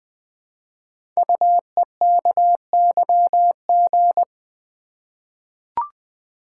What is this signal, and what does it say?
Morse code, heard as a keyed tone, about 20 wpm, 700 Hz. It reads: UEKYG
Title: Code Groups - 5 Letters - 20wpm